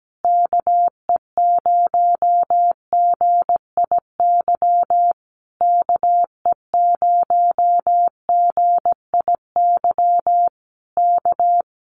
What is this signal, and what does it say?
Morse code, heard as a keyed tone, about 17 wpm, 700 Hz. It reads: KE0GIY KE0GIY K
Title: Common QSO Phrases Encore 17wpm